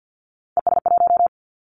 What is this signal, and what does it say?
Morse code, heard as a keyed tone, about 48 wpm, 700 Hz. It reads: ES1
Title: Callsign Prefixes for Top 100 DXCC - 50wpm